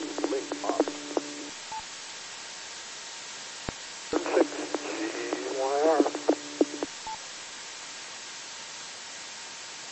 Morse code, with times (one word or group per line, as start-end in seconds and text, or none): none